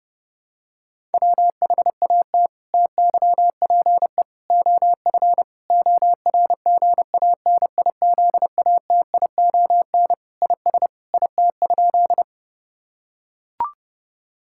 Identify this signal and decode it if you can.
Morse code, heard as a keyed tone, about 30 wpm, 700 Hz. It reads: WHAT TYPE OF ORGANIZATION IS IT?
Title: Sentences from Top 1000 Words - Review Part 5 - 30wpm